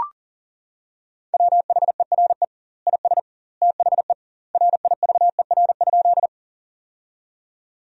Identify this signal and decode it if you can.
Morse code, heard as a keyed tone, about 40 wpm, 700 Hz. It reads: WHERE IS THE RIVER?
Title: 2x Word-Spacing - Sentences from Top 300 Words - 40wpm